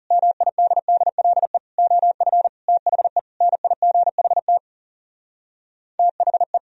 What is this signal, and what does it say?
Morse code, heard as a keyed tone, about 38 wpm, 700 Hz. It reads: MIDDLE OF THE NIGHT THE